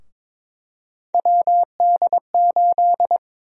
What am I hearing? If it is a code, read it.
Morse code, heard as a keyed tone, about 23 wpm, 700 Hz. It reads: WD8
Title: USA Call Sign Prefixes - 22wpm